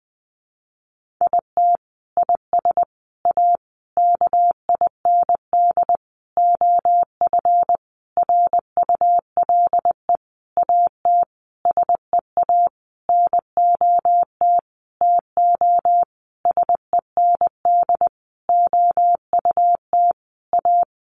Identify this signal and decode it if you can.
Morse code, heard as a keyed tone, about 20 wpm, 700 Hz. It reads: IT IS A KIND OF RULE AT SEA NOT TO SEND OUT A
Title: Sentences from Top 1500 Words 20wpm